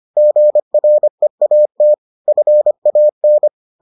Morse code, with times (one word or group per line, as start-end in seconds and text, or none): 0.2-1.9 GREAT
2.3-3.5 FAN